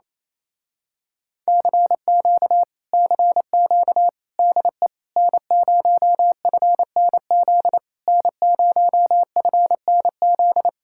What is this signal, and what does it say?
Morse code, heard as a keyed tone, about 27 wpm, 700 Hz. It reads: CQ CQ DE N0FNZ N0FNZ
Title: Speed-Racing Common QSO Phrases Encore - 28wpm